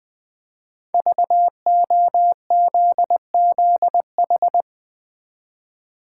Morse code, heard as a keyed tone, about 20 wpm, 700 Hz. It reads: VOZZH
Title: Code Groups - 5 Letter:Numbers - 20wpm